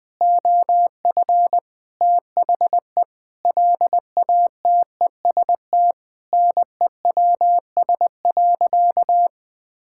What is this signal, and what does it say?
Morse code, heard as a keyed tone, about 20 wpm, 700 Hz. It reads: OF THE LATEST NEWS.
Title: Sentences with 6-Word Phrases - 20wpm